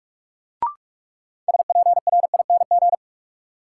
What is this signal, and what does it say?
Morse code, heard as a keyed tone, about 46 wpm, 700 Hz. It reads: SPRING
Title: Sentences from Top 1200 Words - New Word Drill - 45wpm